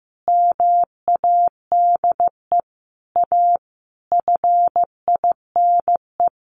Morse code, heard as a keyed tone, about 15 wpm, 700 Hz. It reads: MADE A FINE
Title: Sentences from Top 1100 Words 15wpm